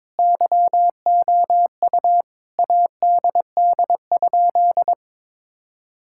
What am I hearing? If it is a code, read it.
Morse code, heard as a keyed tone, about 22 wpm, 700 Hz. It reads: YOU ADD?
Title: Sentences from Top 1000 Words - Review Part 5 - 22wpm